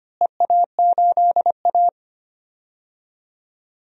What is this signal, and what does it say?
Morse code, heard as a keyed tone, about 25 wpm, 700 Hz. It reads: EA8A
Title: DXCC-100 Call Signs - 4 Characters - Repeated 3x - 25wpm